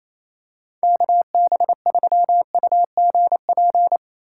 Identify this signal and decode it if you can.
Morse code, heard as a keyed tone, about 28 wpm, 700 Hz. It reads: KB3UGP
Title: USA Call Signs - 6 Characters - 3x Character Spacing - 28wpm